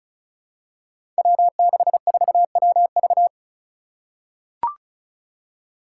Morse code, heard as a keyed tone, about 36 wpm, 700 Hz. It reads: W64WV
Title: Five Challenging Letter-Number Combinations 35wpm